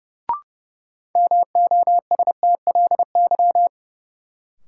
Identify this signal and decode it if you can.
Morse code, heard as a keyed tone, about 31 wpm, 700 Hz. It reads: MOSTLY
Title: Sentences from Top 1300 Words - New Word Drill - 30wpm